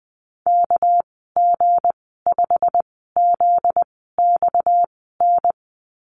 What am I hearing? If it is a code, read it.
Morse code, heard as a keyed tone, about 20 wpm, spaced as for 15 wpm, 700 Hz. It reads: KG5ZXN
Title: 245 Lesson 39g - Letter z - Call Signs 20wpm